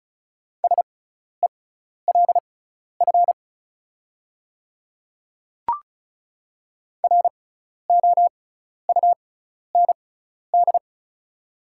Morse code, heard as a keyed tone, about 34 wpm, spaced as for 12 wpm, 700 Hz. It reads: SELF ROUND
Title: 6x Character Spacing - Top 200 Words - 35wpm